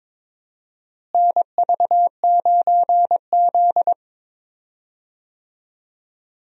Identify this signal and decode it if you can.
Morse code, heard as a keyed tone, about 22 wpm, 700 Hz. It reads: NV9Z
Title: USA Call Signs - 4 Characters - Repeated 2x - 22wpm